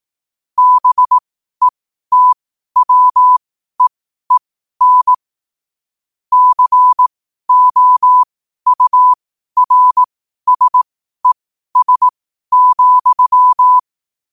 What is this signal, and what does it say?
Morse code, heard as a keyed tone, about 18 wpm, spaced as for 12 wpm, 1000 Hz. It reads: BETWEEN COURSES,
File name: qotdfiles / 12wpm